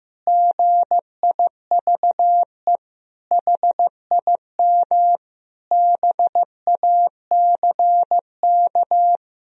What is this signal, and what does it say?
Morse code, heard as a keyed tone, about 16 wpm, 700 Hz. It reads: GIVE HIM BACK
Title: Sentences from Top 1000 Words - Review Part 4 - 15wpm